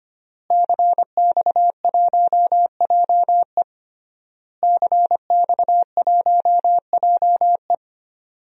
Morse code, heard as a keyed tone, about 25 wpm, 700 Hz. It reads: CX1JE CX1JE
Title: DXCC-100 Call Signs - 5 Characters - Repeated 3x - 25wpm